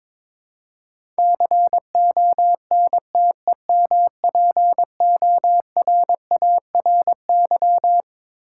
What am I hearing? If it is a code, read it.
Morse code, heard as a keyed tone, about 22 wpm, 700 Hz. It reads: CONTEMPORARY
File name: Twelve Letter Words - 22wpm